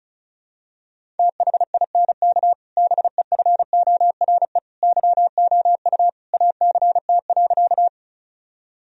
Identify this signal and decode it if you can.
Morse code, heard as a keyed tone, about 36 wpm, 700 Hz. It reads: THINK BEFORE YOU ACT.
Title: Sentences from Top 1000 Words - Review Part 1 - 35wpm